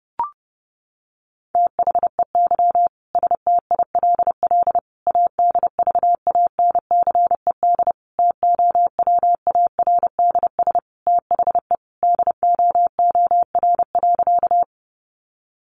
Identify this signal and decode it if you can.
Morse code, heard as a keyed tone, about 30 wpm, 700 Hz. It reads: THEY STILL ADVANCED TOWARDS THE DOOR.
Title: Sentences from Top 1200 Words 30wpm